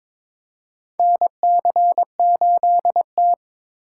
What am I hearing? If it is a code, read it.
Morse code, heard as a keyed tone, about 22 wpm, 700 Hz. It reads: NC8T
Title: USA Call Signs - 4 Characters - 2x Character Spacing - 22wpm